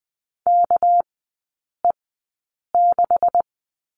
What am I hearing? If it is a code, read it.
Morse code, heard as a keyed tone, about 20 wpm, spaced as for 8 wpm, 700 Hz. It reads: KE6
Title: 218 Lesson 35e - Number 0 - Three Characters 20wpm